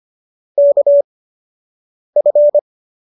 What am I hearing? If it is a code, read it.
Morse code, heard as a keyed tone, about 25 wpm, 570 Hz. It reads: K F
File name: qso214 13